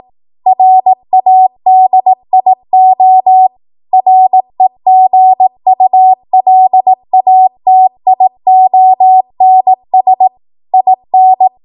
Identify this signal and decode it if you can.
Morse code, heard as a keyed tone, about 18 wpm, 750 Hz. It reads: RADIO REGULATIONS IN
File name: Morse / 18WPM